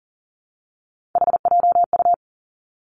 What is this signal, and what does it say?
Morse code, heard as a keyed tone, about 40 wpm, 700 Hz. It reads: HJU
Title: Three Challenging Letter-Number Combinations 40wpm